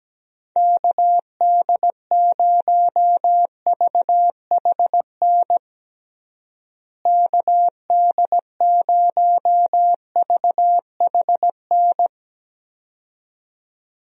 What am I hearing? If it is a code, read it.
Morse code, heard as a keyed tone, about 17 wpm, 700 Hz. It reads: KD0VHN KD0VHN
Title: USA Call Signs - 6 Characters - Repeated 2x - 17wpm